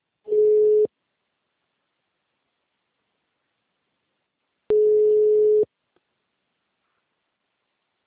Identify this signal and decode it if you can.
Ringback tone (Europe, 425 Hz).
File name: out- -2015- - - .46